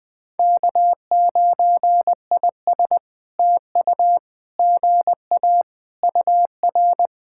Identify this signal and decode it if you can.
Morse code, heard as a keyed tone, about 20 wpm, 700 Hz. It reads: K9IS TU GA UR
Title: POTA QSO Practice - 20wpm